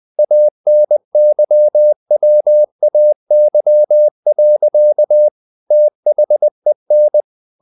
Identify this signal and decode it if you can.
Morse code, heard as a keyed tone, about 20 wpm, 600 Hz. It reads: ANYWAY. THEN
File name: the old man and the sea part 2 20wpm